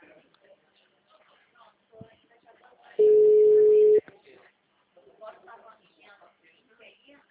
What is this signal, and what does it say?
Ringback tone (Europe, 425 Hz), with speech.